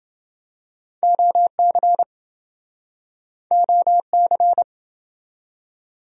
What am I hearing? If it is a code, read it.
Morse code, heard as a keyed tone, about 28 wpm, 700 Hz. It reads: OC OC